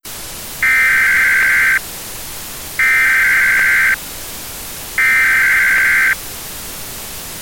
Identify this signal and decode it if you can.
Emergency Alert System (SAME) tones (header bursts), heard as harsh buzzing bursts. It is heard from 0.5 to 6.2 s.